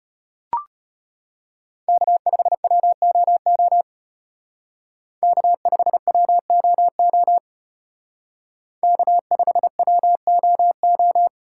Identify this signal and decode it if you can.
Morse code, heard as a keyed tone, about 31 wpm, 700 Hz. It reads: K5WOO K5WOO K5WOO
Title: Speed Racing - USA 1x3 Call Signs 25wpm